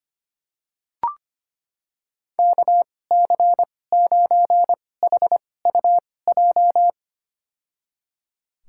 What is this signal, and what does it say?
Morse code, heard as a keyed tone, about 25 wpm, spaced as for 18 wpm, 700 Hz. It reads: KC9HUJ
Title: USA Call Signs - 2x Character Spacing - 25wpm